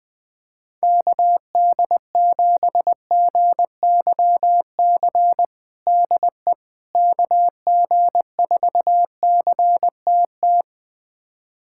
Morse code, heard as a keyed tone, about 20 wpm, 700 Hz. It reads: KD7GYC DE KG4CTT
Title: USA Callsign DE Callsign - 6 Characters - 20wpm